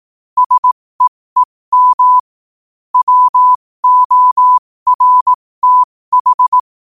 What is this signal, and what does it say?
Morse code, heard as a keyed tone, about 18 wpm, spaced as for 15 wpm, 1000 Hz. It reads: SEEM WORTH
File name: qotdfiles / 15wpm